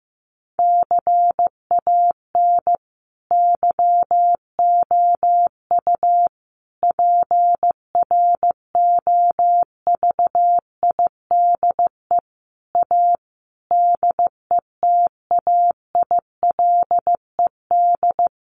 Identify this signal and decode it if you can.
Morse code, heard as a keyed tone, about 15 wpm, 700 Hz. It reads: CAN YOU PROVIDE A DETAILED
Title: Sentences from Top 1500 Words 15wpm